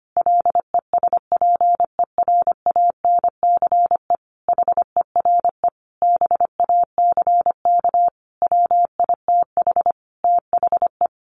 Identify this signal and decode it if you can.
Morse code, heard as a keyed tone, about 25 wpm, 700 Hz. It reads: LESPERANCE HERE BACK WITH THE
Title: A Sound Of Thunder 25wpm